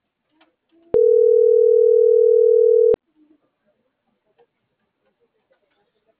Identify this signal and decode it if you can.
Ringback tone (North America), 440 + 480 Hz.